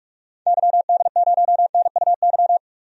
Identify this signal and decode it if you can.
Morse code, heard as a keyed tone, about 46 wpm, 700 Hz. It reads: YD0NUY
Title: DXCC-100 Call Signs - 6 Characters - 45wpm